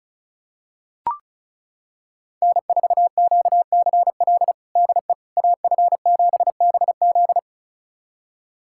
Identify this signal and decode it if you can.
Morse code, heard as a keyed tone, about 34 wpm, 700 Hz. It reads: N4QCL DE AF7BZ
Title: USA Callsign DE Callsign - 5 Characters - 35wpm